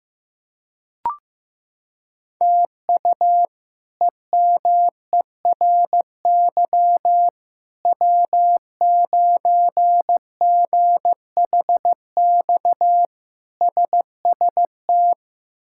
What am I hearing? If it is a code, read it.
Morse code, heard as a keyed tone, about 16 wpm, 700 Hz. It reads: TU EMERY W9GHX SST
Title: K1USN SST QSO Practice - 15wpm